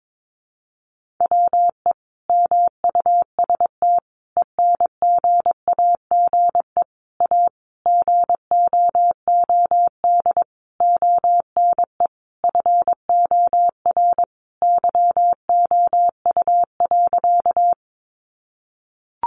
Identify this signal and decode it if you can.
Morse code, heard as a keyed tone, about 22 wpm, 700 Hz. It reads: WE MUST ENGAGE A GOOD ONE FOR YOU.
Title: Sentences from Top 1500 Words 22wpm